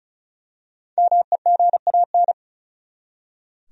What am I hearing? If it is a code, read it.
Morse code, heard as a keyed tone, about 36 wpm, 700 Hz. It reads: MEGAN
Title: USA Common Names - Max 5 Letters - 35wpm